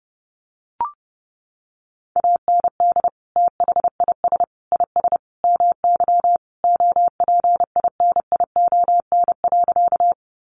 Morse code, heard as a keyed tone, about 28 wpm, 700 Hz. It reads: AND THIS IS MY OPINION.
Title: Sentences from Top 1200 Words 30wpm